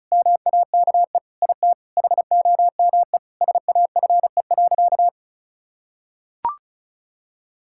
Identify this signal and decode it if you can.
Morse code, heard as a keyed tone, about 34 wpm, 700 Hz. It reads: MAKE IT HOME SAFE.
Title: Sentences from Top 1600 Words 35wpm